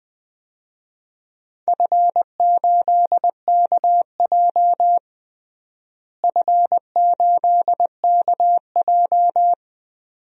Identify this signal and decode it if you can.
Morse code, heard as a keyed tone, about 20 wpm, 700 Hz. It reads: F8KJ F8KJ
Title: DXCC-100 Call Signs - 4 Characters - Repeated 2x - 20wpm